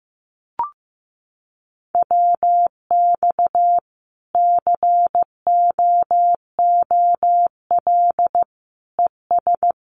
Morse code, heard as a keyed tone, about 15 wpm, 700 Hz. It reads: WX COOL ES
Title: Common QSO Phrases 15wpm